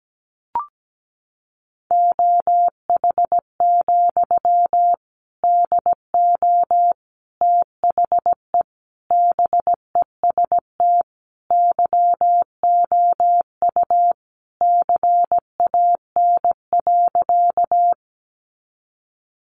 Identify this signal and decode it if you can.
Morse code, heard as a keyed tone, about 17 wpm, 700 Hz. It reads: OH, DO THE BEST YOU CAN.
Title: Sentences from Top 500 Words 17wpm